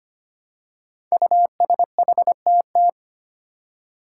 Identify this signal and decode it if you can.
Morse code, heard as a keyed tone, about 25 wpm, 700 Hz. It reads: USHTT
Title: Five Challenging Letter-Number Combinations 25wpm